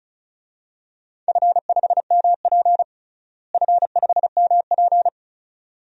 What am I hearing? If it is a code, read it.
Morse code, heard as a keyed tone, about 34 wpm, 700 Hz. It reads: F5MP F5MP
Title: DXCC-100 Call Signs - 4 Characters - Repeated 2x - 35wpm